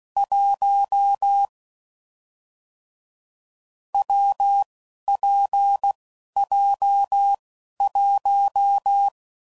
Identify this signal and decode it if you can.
Morse code, heard as a keyed tone, about 16 wpm, spaced as for 11 wpm, 800 Hz. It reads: 1 WPJ1
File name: Combinations / Difficult Combos / WPJ WPJ1